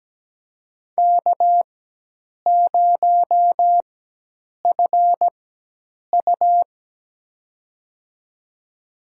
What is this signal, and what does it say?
Morse code, heard as a keyed tone, about 17 wpm, spaced as for 8 wpm, 700 Hz. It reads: K0FU
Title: USA Call Signs - 4x Character Spacing - 17wpm